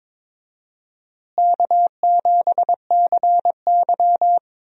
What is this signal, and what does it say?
Morse code, heard as a keyed tone, about 22 wpm, 700 Hz. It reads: K7CY
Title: USA Call Signs - 4 Characters - Repeated 3x - 22wpm